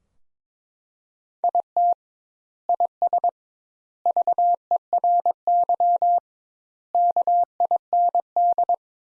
Morse code, heard as a keyed tone, about 22 wpm, 700 Hz. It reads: IT IS VERY KIND
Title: Sentences Top 200 Words - 22wpm